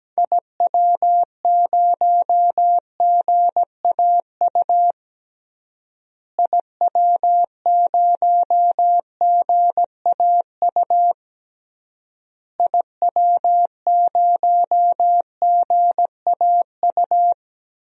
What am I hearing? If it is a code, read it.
Morse code, heard as a keyed tone, about 17 wpm, 700 Hz. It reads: IW0GAU IW0GAU IW0GAU
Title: DXCC-100 Call Signs - 6 Characters - Repeated 3x - 17wpm